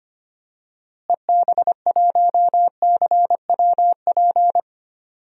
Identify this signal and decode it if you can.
Morse code, heard as a keyed tone, about 25 wpm, 700 Hz. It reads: EB1CWP
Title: DXCC-100 Call Signs - 6 Characters - Repeated 3x - 25wpm